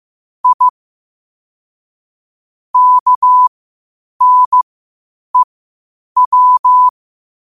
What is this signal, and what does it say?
Morse code, heard as a keyed tone, about 15 wpm, spaced as for 8 wpm, 1000 Hz. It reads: I KNEW
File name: qotdfiles / 8wpm